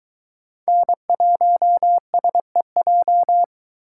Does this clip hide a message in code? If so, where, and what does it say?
Yes: Morse code from 0.7 to 3.4 s, heard as a keyed tone, about 23 wpm, 700 Hz. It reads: N1SEJ